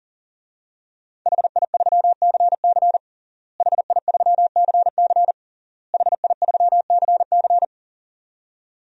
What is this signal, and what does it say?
Morse code, heard as a keyed tone, about 40 wpm, 700 Hz. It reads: HI3CC HI3CC HI3CC
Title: DXCC-100 Call Signs - 5 Characters - Repeated 3x - 40wpm